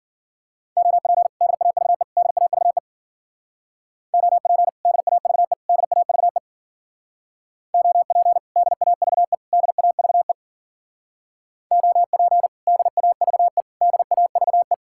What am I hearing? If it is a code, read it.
Morse code, heard as a keyed tone, about 49 wpm, 700 Hz. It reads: OP DAVE DAVE OP DAVE DAVE OP DAVE DAVE OP DAVE DAVE
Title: Speed-Racing Common QSO Phrases - 40wpm